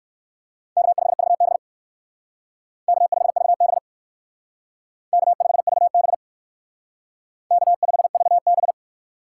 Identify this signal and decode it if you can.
Morse code, heard as a keyed tone, about 64 wpm, 700 Hz. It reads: K5VB K5VB K5VB K5VB